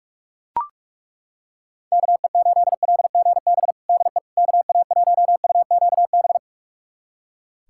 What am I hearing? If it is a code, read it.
Morse code, heard as a keyed tone, about 41 wpm, 700 Hz. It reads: KE8LGB DE KA1UQB
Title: USA Callsign DE Callsign - 6 Characters - 45wpm